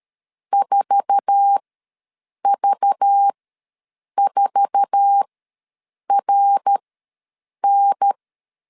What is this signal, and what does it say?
Morse code, heard as a keyed tone, about 13 wpm, spaced as for 7 wpm, 800 Hz. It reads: 4V4RN
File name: LEZIONI / 8 YJ45 / 8 YJ4 ws 12wpm 3249